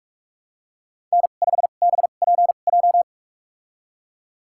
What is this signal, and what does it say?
Morse code, heard as a keyed tone, about 46 wpm, spaced as for 32 wpm, 700 Hz. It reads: N5BPJ
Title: USA Call Signs - 5 Characters - 2x Character Spacing - 45wpm